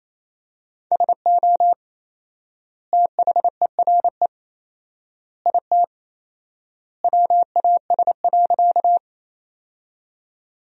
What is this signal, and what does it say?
Morse code, heard as a keyed tone, about 28 wpm, 700 Hz. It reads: SO THERE IT WAS.
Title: 4x Word-Spacing - Sentences from Top 100 Words - 28wpm